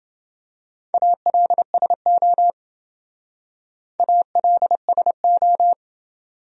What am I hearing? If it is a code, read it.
Morse code, heard as a keyed tone, about 27 wpm, 700 Hz. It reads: ALSO ALSO